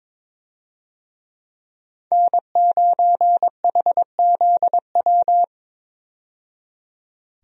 Morse code, heard as a keyed tone, about 21 wpm, 700 Hz. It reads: N9HZW
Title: USA Call Signs - 5 Characters - 22wpm